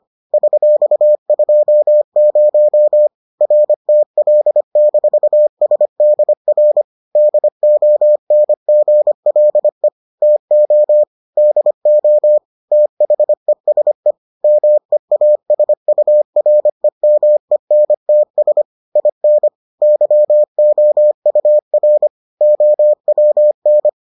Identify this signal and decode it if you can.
Morse code, heard as a keyed tone, about 25 wpm, 600 Hz. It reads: $20 RTL-SDR DONGLE TO DO THESE MEASUREMENTS IN YOUR OWN